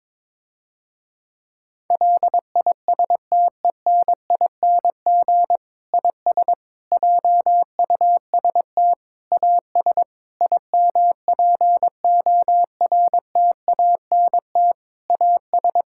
Morse code, heard as a keyed tone, about 22 wpm, 700 Hz. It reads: LISTENING IS JUST AS IMPORTANT AS
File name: Difficult Sentences with Hints - 22wpm